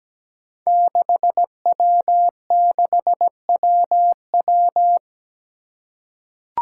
Morse code, heard as a keyed tone, about 17 wpm, 700 Hz. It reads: 6W6WW
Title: Five Challenging Letter-Number Combinations 17wpm